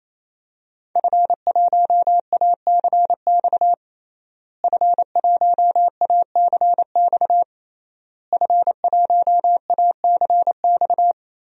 Morse code, heard as a keyed tone, about 28 wpm, 700 Hz. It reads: F1ACX F1ACX F1ACX
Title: DXCC-100 Call Signs - 5 Characters - Repeated 3x - 28wpm